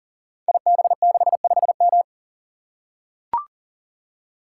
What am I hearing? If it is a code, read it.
Morse code, heard as a keyed tone, about 39 wpm, 690 Hz. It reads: IB65M
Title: Code Groups - 5 Letter:Numbers - 40wpm